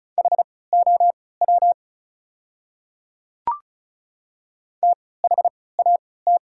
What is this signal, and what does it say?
Morse code, heard as a keyed tone, about 34 wpm, spaced as for 20 wpm, 700 Hz. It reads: HOW THAT
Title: 3x Character Spacing - Top 100 Words - 35wpm